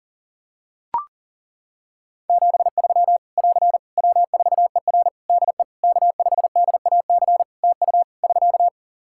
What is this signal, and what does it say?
Morse code, heard as a keyed tone, about 40 wpm, 700 Hz. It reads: 73 <AR> W4ER DE K5DAC TU <SK>
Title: K1USN SST QSO Practice 40wpm